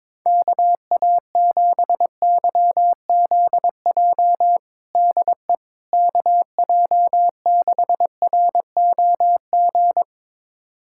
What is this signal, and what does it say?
Morse code, heard as a keyed tone, about 22 wpm, 700 Hz. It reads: KA7YZJ DE KJ6ROG
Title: USA Callsign DE Callsign - 6 Characters - 22wpm